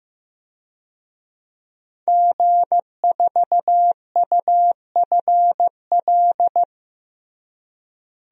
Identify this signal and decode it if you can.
Morse code, heard as a keyed tone, about 15 wpm, 700 Hz. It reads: G4UFL
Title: DXCC-100 Call Signs - 5 Characters - Repeated 3x - 15wpm